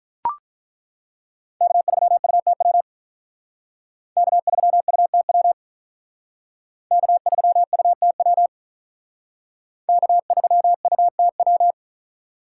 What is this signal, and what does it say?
Morse code, heard as a keyed tone, about 43 wpm, 700 Hz. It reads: K3UTW K3UTW K3UTW K3UTW
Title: Speed Racing - USA 1x3 Call Signs 35wpm